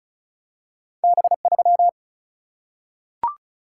Morse code, heard as a keyed tone, about 34 wpm, 700 Hz. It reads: B3
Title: Single-Letter Single-Number 35wpm